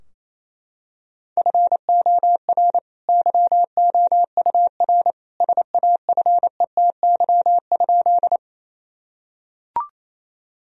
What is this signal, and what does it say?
Morse code, heard as a keyed tone, about 28 wpm, 700 Hz. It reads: FOR YOUR SAFETY?
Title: Sentences from Top 1000 Words - Review Part 4 - 28wpm